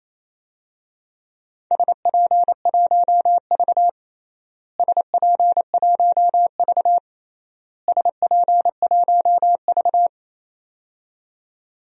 Morse code, heard as a keyed tone, about 28 wpm, 700 Hz. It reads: SP1V SP1V SP1V
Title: DXCC-100 Call Signs - 4 Characters - Repeated 3x - 28wpm